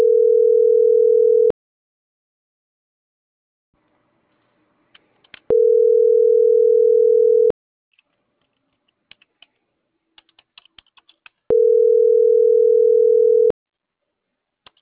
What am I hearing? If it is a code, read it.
Ringback tone (North America), 440 + 480 Hz.